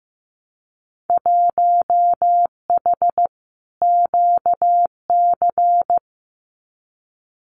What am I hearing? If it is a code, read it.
Morse code, heard as a keyed tone, about 15 wpm, 700 Hz. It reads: 1H QC
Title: Winter Field Day Exchanges 15wpm